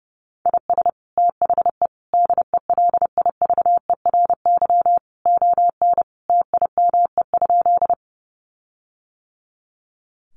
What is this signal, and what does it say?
Morse code, heard as a keyed tone, about 30 wpm, 700 Hz. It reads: IS THE DELIVERY ON TIME?
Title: Sentences from Top 1500 Words 30wpm